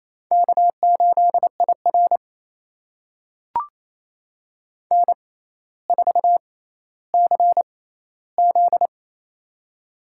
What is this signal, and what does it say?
Morse code, heard as a keyed tone, about 28 wpm, spaced as for 14 wpm, 700 Hz. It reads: K8IR N4CZ
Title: USA Call Signs - 4 Characters - 6x Character Spacing - 28wpm